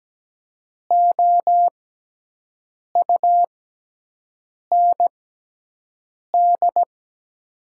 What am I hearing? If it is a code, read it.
Morse code, heard as a keyed tone, about 17 wpm, spaced as for 6 wpm, 700 Hz. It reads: OUND
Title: 6x Character Spacing - Top 500 Words - 17wpm